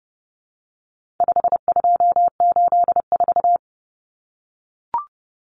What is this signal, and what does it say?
Morse code, heard as a keyed tone, about 30 wpm, 700 Hz. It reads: 5284
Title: Common Cut Numbers - 4-Digit 30wpm